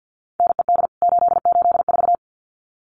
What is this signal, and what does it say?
Morse code, heard as a keyed tone, about 50 wpm, 700 Hz. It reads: NED 884
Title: CWT Exchange 50wpm